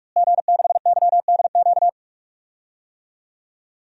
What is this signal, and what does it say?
Morse code, heard as a keyed tone, about 44 wpm, 700 Hz. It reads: G6YDQ
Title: DXCC-100 Call Signs - 5 Characters - Repeated 2x - 45wpm